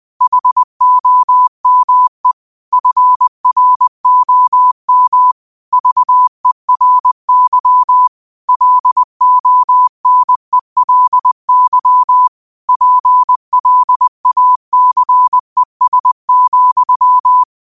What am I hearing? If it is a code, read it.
Morse code, heard as a keyed tone, about 20 wpm, 1000 Hz. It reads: HOME FROM VERY LONELY PLACES,
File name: qotdfiles / 20wpm